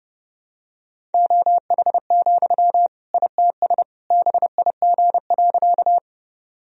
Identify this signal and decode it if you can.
Morse code, heard as a keyed tone, about 32 wpm, 700 Hz. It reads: OH, ITS BIG.
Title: Sentences from Top 1000 Words - Review Part 1 - 30wpm